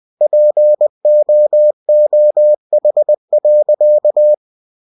Morse code, heard as a keyed tone, about 20 wpm, 600 Hz. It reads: POOH.
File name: cw-books / winnie-the-pooh / winnie-20wpm-page-0076